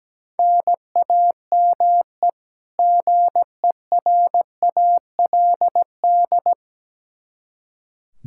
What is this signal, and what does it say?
Morse code, heard as a keyed tone, about 17 wpm, 700 Hz. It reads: NAME GERALD
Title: Practice QSO - Exchange 1 - Beginning - 17wpm